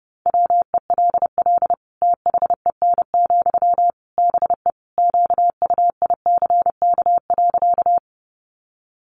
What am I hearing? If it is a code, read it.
Morse code, heard as a keyed tone, about 30 wpm, 700 Hz. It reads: WELL THEN, BE QUICK.
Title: Sentences from Top 500 Words 30wpm